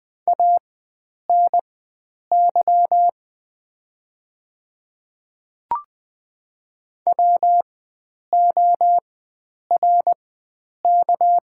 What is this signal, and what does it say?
Morse code, heard as a keyed tone, about 20 wpm, spaced as for 9 wpm, 700 Hz. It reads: ANY WORK
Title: 4x Character Spacing - Top 300 Words - 20wpm